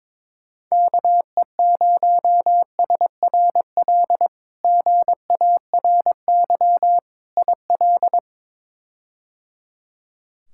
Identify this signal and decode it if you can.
Morse code, heard as a keyed tone, about 22 wpm, 700 Hz. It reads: KE0SRL GARY IL
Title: K1USN SST QSO Practice - 22wpm